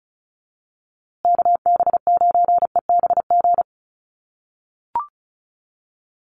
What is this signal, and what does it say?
Morse code, heard as a keyed tone, about 34 wpm, 700 Hz. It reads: KB9EBG
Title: USA Call Signs Repeated 2x 35wpm